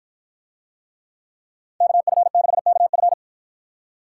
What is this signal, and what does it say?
Morse code, heard as a keyed tone, about 50 wpm, 700 Hz. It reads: KF6KF